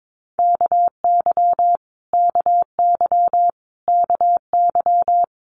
Morse code, heard as a keyed tone, about 22 wpm, 700 Hz. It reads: KY KY KY
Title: Multi-letter Phonemes Repeated 3x 22wpm